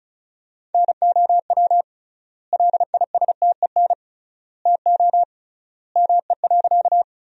Morse code, heard as a keyed tone, about 36 wpm, 700 Hz. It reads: NOW LISTEN TO ME.
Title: 3x Word-Spacing - Sentences from Top 400 Words - 35wpm